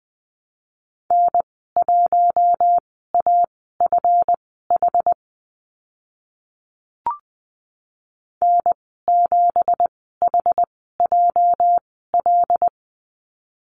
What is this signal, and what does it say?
Morse code, heard as a keyed tone, about 20 wpm, spaced as for 15 wpm, 700 Hz. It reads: N1AFH N7HJL
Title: 213 Lesson 34g - Number 8 - Call Signs 20wpm